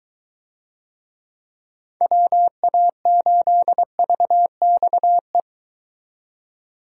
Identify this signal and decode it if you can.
Morse code, heard as a keyed tone, about 23 wpm, 700 Hz. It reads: WA8VXE